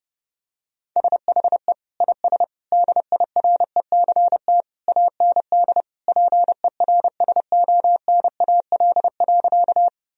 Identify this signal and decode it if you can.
Morse code, heard as a keyed tone, about 30 wpm, 700 Hz. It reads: SHE IS DIRECT AND PERSONAL.
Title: Sentences from Top 1000 Words - Review Part 4 - 30wpm